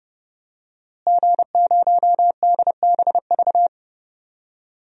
Morse code, heard as a keyed tone, about 30 wpm, 700 Hz. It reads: G0DBV
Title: DXCC-100 Call Signs - 5 Characters - Repeated 2x - 30wpm